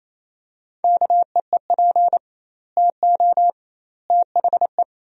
Morse code, heard as a keyed tone, about 28 wpm, 700 Hz. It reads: KEEP TO THE
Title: 2x Word-Spacing - Sentences from Top 300 Words - 28wpm